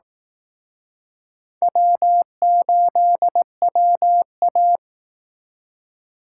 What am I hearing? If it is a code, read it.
Morse code, heard as a keyed tone, about 18 wpm, 700 Hz. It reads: W8WA
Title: Speed Racing - USA Call Signs 15wpm